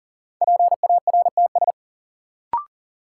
Morse code, heard as a keyed tone, about 40 wpm, 700 Hz. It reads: PARTS
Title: Sentences from Top 1000 Words - Review Part 6 - New Word Drill - 40wpm